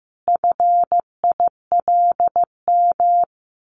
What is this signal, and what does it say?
Morse code, heard as a keyed tone, about 15 wpm, 700 Hz. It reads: FILM
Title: A Sound Of Thunder 15wpm